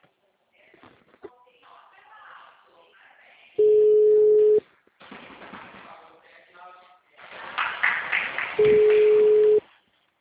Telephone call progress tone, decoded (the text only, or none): ringback tone